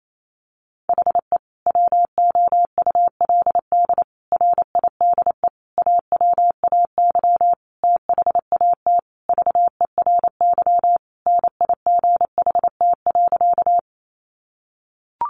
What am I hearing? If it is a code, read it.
Morse code, heard as a keyed tone, about 28 wpm, 700 Hz. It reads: HE WOULD RIDE AWAY THAT VERY NIGHT.
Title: Sentences from Top 1200 Words 28wpm